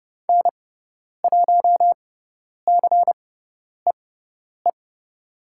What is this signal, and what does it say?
Morse code, heard as a keyed tone, about 30 wpm, spaced as for 10 wpm, 700 Hz. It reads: N1CEE
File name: Lesson 17g - Letter u - Call Signs 30wpm